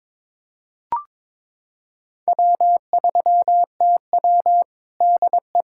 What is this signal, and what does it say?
Morse code, heard as a keyed tone, about 23 wpm, 700 Hz. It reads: W3TW DE
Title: K1USN SST QSO Practice 22wpm